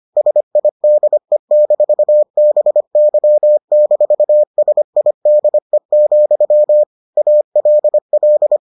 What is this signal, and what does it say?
Morse code, heard as a keyed tone, about 25 wpm, 600 Hz. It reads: SIDE-BY-SIDE, ALL